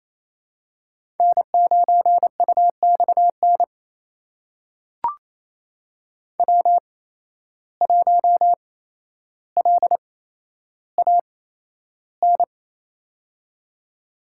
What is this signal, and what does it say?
Morse code, heard as a keyed tone, about 28 wpm, spaced as for 12 wpm, 700 Hz. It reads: N9UXN W1LAN
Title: USA Call Signs - 5 Characters - 8x Character Spacing - 28wpm